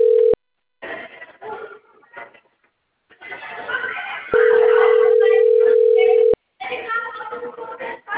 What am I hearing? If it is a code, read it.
Ringback tone (North America), 440 + 480 Hz, with speech.
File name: rg-1600-unknown- - - .64675